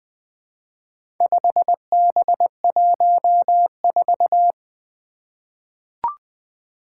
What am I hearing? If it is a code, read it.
Morse code, heard as a keyed tone, about 21 wpm, 700 Hz. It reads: 5B14
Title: Four Challenging Letter-Number Combinations 20wpm